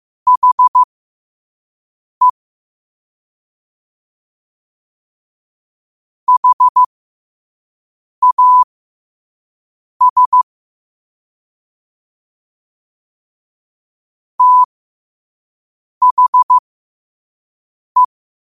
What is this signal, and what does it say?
Morse code, heard as a keyed tone, about 15 wpm, spaced as for 5 wpm, 1000 Hz. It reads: HE HAS THE
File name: qotdfiles / 5wpm